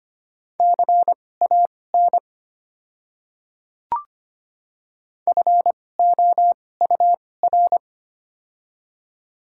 Morse code, heard as a keyed tone, about 25 wpm, spaced as for 18 wpm, 700 Hz. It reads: CAN FOUR
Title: 2x Character Spacing - Top 300 Words - 25wpm